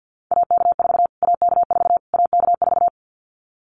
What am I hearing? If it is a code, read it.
Morse code, heard as a keyed tone, about 50 wpm, 700 Hz. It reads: AK4 AK4 AK4
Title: Callsign Prefixes for Top 25 DXCC - Repeated 3x - 50wpm